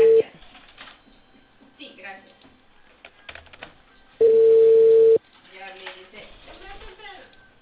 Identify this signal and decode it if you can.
Ringback tone (China, 450 Hz), with speech.